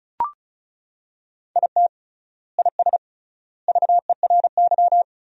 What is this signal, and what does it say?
Morse code, heard as a keyed tone, about 36 wpm, 700 Hz. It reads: IT IS VERY
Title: 3x Word-Spacing - Sentences from Top 200 Words - 35wpm